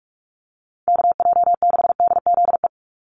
Morse code, heard as a keyed tone, about 44 wpm, 700 Hz. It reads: KJ6DZE
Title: USA Call Signs 45wpm